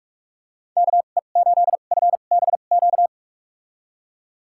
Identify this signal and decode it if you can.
Morse code, heard as a keyed tone, about 47 wpm, spaced as for 33 wpm, 700 Hz. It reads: KE8FBQ
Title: USA Call Signs - 6 Characters - 2x Character Spacing - 45wpm